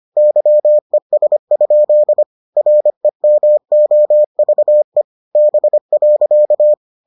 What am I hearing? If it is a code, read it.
Morse code, heard as a keyed tone, about 25 wpm, 600 Hz. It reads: YES? REMOVE B.